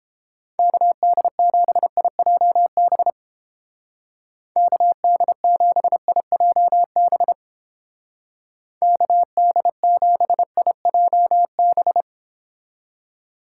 Morse code, heard as a keyed tone, about 29 wpm, 700 Hz. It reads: KD7IJB KD7IJB KD7IJB